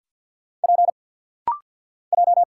Morse code, heard as a keyed tone, about 50 wpm, 700 Hz. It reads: P J
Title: Single Letter-Number - Rapid-Fire 50wpm